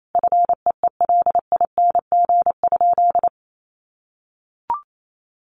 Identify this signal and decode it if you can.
Morse code, heard as a keyed tone, about 28 wpm, 700 Hz. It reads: FEELING?
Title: Sentences from Top 1200 Words 28wpm